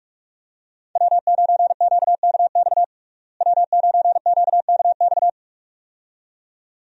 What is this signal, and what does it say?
Morse code, heard as a keyed tone, about 41 wpm, 700 Hz. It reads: W9QKX W9QKX
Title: USA Call Signs - 5 Characters - Repeated 2x - 45wpm